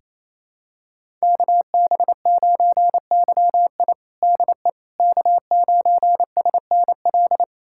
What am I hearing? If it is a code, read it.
Morse code, heard as a keyed tone, about 27 wpm, 700 Hz. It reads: KB9YI DE K9SNL
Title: USA Callsign DE Callsign - 5 Characters - 28wpm